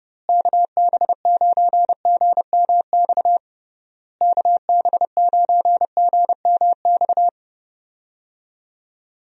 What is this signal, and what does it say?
Morse code, heard as a keyed tone, about 30 wpm, 700 Hz. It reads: KB9GMX KB9GMX
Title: USA Call Signs - 6 Characters - Repeated 3x - 30wpm